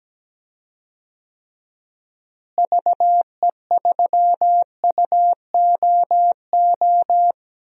Morse code, heard as a keyed tone, about 17 wpm, 700 Hz. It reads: VE3UOO
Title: DXCC-100 Call Signs - 6 Characters - Repeated 3x - 17wpm